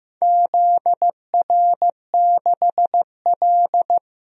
Morse code, heard as a keyed tone, about 15 wpm, 700 Hz. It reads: ZR6L
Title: DXCC-100 Call Signs - 4 Characters - Repeated 2x - 15wpm